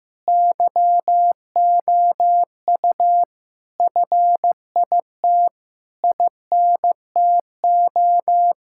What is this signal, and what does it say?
Morse code, heard as a keyed tone, about 15 wpm, 700 Hz. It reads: YOU FIT INTO
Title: Sentences from Top 1000 Words - Review Part 6 - 15wpm